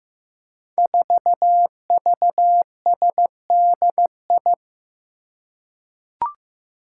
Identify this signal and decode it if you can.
Morse code, heard as a keyed tone, about 15 wpm, 700 Hz. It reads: 4VSDI
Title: Five Challenging Letter-Number Combinations 15wpm